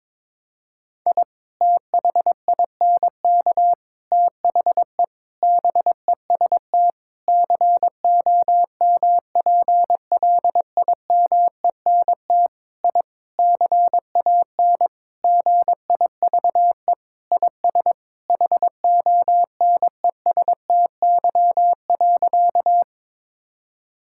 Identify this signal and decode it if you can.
Morse code, heard as a keyed tone, about 22 wpm, 700 Hz. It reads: I THINK THE BEST COMPLIMENT I CAN GIVE IS HONESTY.
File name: practice / 22wpm / Sentences with 8-Word Phrases - 22wpm